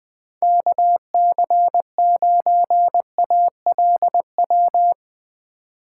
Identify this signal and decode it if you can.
Morse code, heard as a keyed tone, about 20 wpm, 700 Hz. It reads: KC9ALW
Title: Speed Racing - USA Call Signs 20wpm